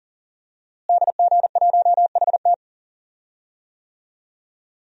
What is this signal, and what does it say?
Morse code, heard as a keyed tone, about 40 wpm, 700 Hz. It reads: DG1HT
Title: DXCC-100 Call Signs - 5 Characters - Repeated 3x - 40wpm